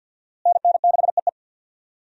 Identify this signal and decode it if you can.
Morse code, heard as a keyed tone, about 51 wpm, 700 Hz. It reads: NN6EE
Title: USA Call Signs - 5 Characters - Repeated 2x - 50wpm